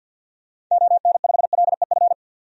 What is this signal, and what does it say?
Morse code, heard as a keyed tone, about 53 wpm, 700 Hz. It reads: ON5LEF
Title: DXCC-100 Call Signs - 6 Characters - 50wpm